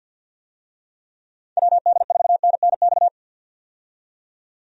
Morse code, heard as a keyed tone, about 50 wpm, 700 Hz. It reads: WD4NNX
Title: USA Call Signs - 6 Characters - 50wpm